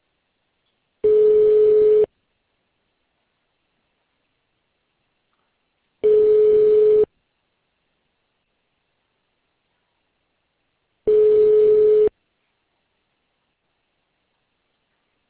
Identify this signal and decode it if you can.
Ringback tone (Europe, 425 Hz).